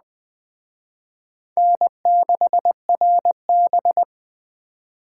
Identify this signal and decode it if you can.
Morse code, heard as a keyed tone, about 20 wpm, 700 Hz. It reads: N6RB
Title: Speed Racing - USA Call Signs 17wpm